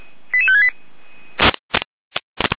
Five-tone selective call (ZVEI), heard as quick runs of five tones, tones of about 70 ms. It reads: 88568